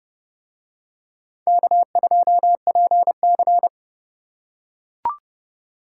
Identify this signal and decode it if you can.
Morse code, heard as a keyed tone, about 30 wpm, 700 Hz. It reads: K2PC